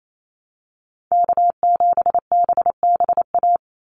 Morse code, heard as a keyed tone, about 28 wpm, 700 Hz. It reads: K7BBA
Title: USA Call Signs 28wpm